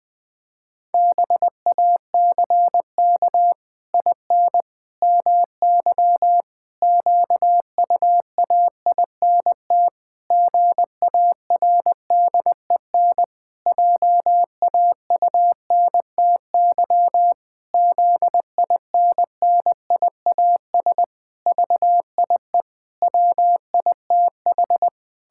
Morse code, heard as a keyed tone, about 20 wpm, 700 Hz. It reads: BACK IN MY QUAINT GARDEN JAUNTY ZINNIAS VIE WITH
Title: Pangram Challenge v2 - 20wpm